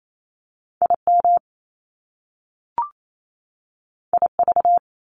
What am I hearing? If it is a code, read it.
Morse code, heard as a keyed tone, about 28 wpm, 700 Hz. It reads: IM IV
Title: Top Two Letter Combinations in Words 28wpm